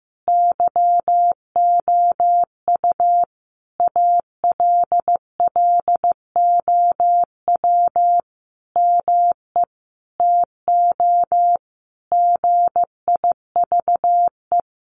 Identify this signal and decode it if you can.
Morse code, heard as a keyed tone, about 15 wpm, 700 Hz. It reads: YOU ALLOW ME TO GIVE
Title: Sentences from Top 1200 Words 15wpm